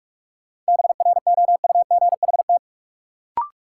Morse code, heard as a keyed tone, about 44 wpm, 700 Hz. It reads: BROUGHT
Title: Sentences from Top 500 Words - New Word Drill - 45wpm